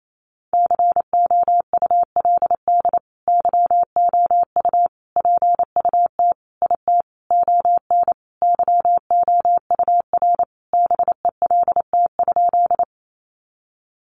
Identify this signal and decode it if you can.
Morse code, heard as a keyed tone, about 28 wpm, 700 Hz. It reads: COULD YOU PUT IT ON YOUR BELT?
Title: Sentences from Top 1400 Words 28wpm